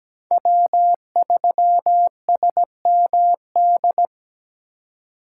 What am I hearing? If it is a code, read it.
Morse code, heard as a keyed tone, about 21 wpm, 700 Hz. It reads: W3SMD
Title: Speed Racing - USA 1x3 Call Signs 17wpm